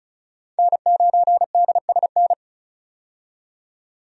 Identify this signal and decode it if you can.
Morse code, heard as a keyed tone, about 34 wpm, 700 Hz. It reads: N9DSN
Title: USA Call Signs - 5 Characters - 35wpm